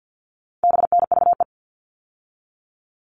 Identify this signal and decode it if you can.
Morse code, heard as a keyed tone, about 50 wpm, 700 Hz. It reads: BNVE
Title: All Cut Numbers - 4-Digit 50wpm